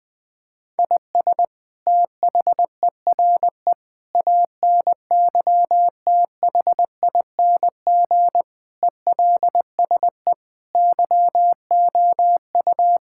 Morse code, heard as a keyed tone, about 20 wpm, 700 Hz. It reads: IS THERE ANYTHING ELSE YOU
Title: Sentences from Top 1600 Words 20wpm